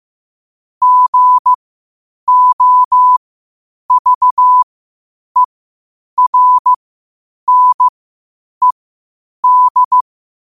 Morse code, heard as a keyed tone, about 15 wpm, spaced as for 8 wpm, 1000 Hz. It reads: GOVERNED
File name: qotdfiles / 8wpm